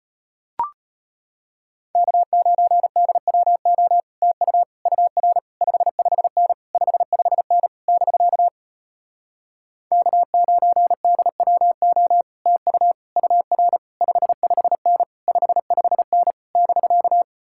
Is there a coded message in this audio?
Yes: Morse code from 1.9 to 17.2 s, heard as a keyed tone, about 33 wpm, 700 Hz. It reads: K9DWO TU UR 55N 55N <BK> K9DWO TU UR 55N 55N <BK>